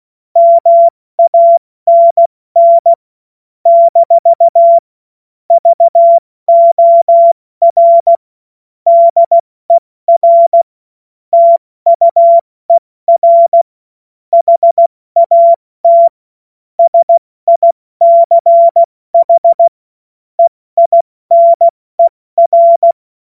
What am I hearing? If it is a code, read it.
Morse code, heard as a keyed tone, about 16 wpm, 680 Hz. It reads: MANN - VOR DER TUER HAT SICH EINER
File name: Morsekurs / Witze / Witz 02 Januar 2024 70 F80